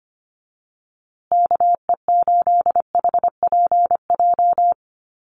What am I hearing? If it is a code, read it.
Morse code, heard as a keyed tone, about 25 wpm, 700 Hz. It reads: KE8HPJ
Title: USA Call Signs Repeated 3x 25wpm